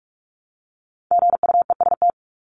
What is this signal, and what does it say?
Morse code, heard as a keyed tone, about 44 wpm, 700 Hz. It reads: GUEST
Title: Top 1501-1750 Words 45wpm